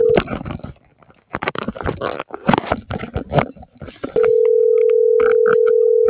Ringback tone (North America), 440 + 480 Hz.